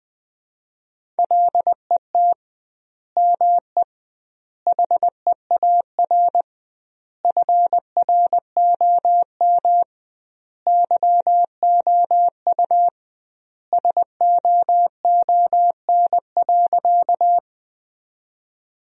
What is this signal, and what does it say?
Morse code, heard as a keyed tone, about 20 wpm, 700 Hz. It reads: LET ME HEAR FROM YOU SOON.
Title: 2x Word-Spacing - Sentences from Top 400 Words - 20wpm